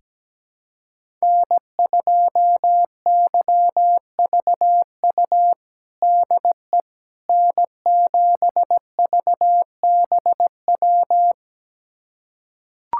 Morse code, heard as a keyed tone, about 17 wpm, 700 Hz. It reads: N2YVU DE N7VBW
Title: USA Callsign DE Callsign 17wpm